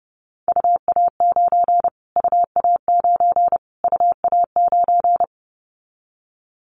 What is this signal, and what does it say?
Morse code, heard as a keyed tone, about 30 wpm, 700 Hz. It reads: UA9 UA9 UA9
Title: Callsign Prefixes for Top 25 DXCC - Repeated 3x - 30wpm